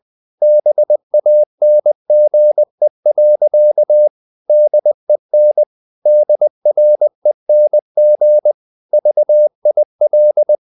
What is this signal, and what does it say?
Morse code, heard as a keyed tone, about 20 wpm, 600 Hz. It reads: BANGE. DEN DRENG VIL
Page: Ferie 20 wpm